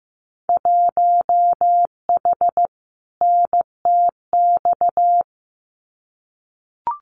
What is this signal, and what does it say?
Morse code, heard as a keyed tone, about 15 wpm, 700 Hz. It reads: 1H NTX
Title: Winter Field Day Exchanges 15wpm